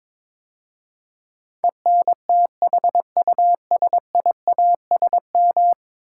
Morse code, heard as a keyed tone, about 22 wpm, 700 Hz. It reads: ENTHUSIASM
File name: practice / 22wpm / 10 Letter Words - First 3 Letters Given - 22wpm